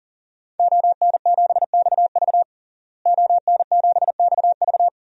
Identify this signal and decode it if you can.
Morse code, heard as a keyed tone, about 40 wpm, 700 Hz. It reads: ON7XV ON7XV
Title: DXCC-100 Call Signs - 5 Characters - Repeated 2x - 40wpm